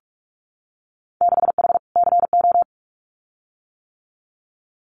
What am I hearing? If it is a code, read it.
Morse code, heard as a keyed tone, about 46 wpm, 700 Hz. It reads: 6H CO
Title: Winter Field Day Exchanges Encore 45wpm